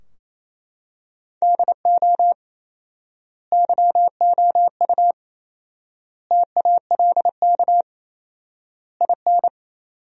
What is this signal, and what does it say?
Morse code, heard as a keyed tone, about 28 wpm, 700 Hz. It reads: DO YOU TALK IN
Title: 4x Word-Spacing - Sentences from Top 500 Words - 28wpm